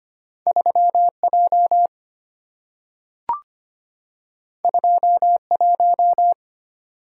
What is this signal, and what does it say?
Morse code, heard as a keyed tone, about 25 wpm, 690 Hz. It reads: 3J 21
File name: Two Letter-Number Combinations - 25wpm